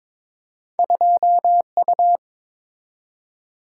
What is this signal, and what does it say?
Morse code, heard as a keyed tone, about 22 wpm, 700 Hz. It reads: 2U
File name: practice / 22wpm / Two Letter-Number Combinations - 22wpm